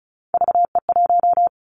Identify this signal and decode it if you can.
Morse code, heard as a keyed tone, about 36 wpm, 700 Hz. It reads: VE1
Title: Callsign Prefixes for Top 50 DXCC - 35wpm